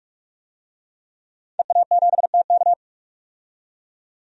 Morse code, heard as a keyed tone, about 46 wpm, 700 Hz. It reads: EA7TK
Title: DXCC-100 Call Signs - 5 Characters - 45wpm